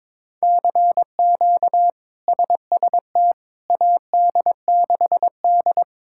Morse code, heard as a keyed tone, about 22 wpm, 700 Hz. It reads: CQ SST AD6D
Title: K1USN SST QSO Practice - 22wpm